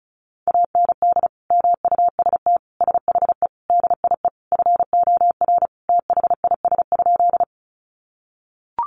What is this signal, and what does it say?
Morse code, heard as a keyed tone, about 34 wpm, 700 Hz. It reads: AND MUST SHE DIE FOR THIS?
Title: Sentences from Top 1300 Words 35wpm